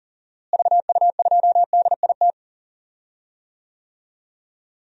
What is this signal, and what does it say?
Morse code, heard as a keyed tone, about 40 wpm, 700 Hz. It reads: VU2DIT
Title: DXCC-100 Call Signs - 6 Characters - Repeated 3x - 40wpm